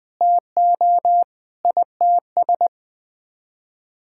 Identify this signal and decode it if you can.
Morse code, heard as a keyed tone, about 21 wpm, 700 Hz. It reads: TO ITS
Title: Sets of 9 Words - 20wpm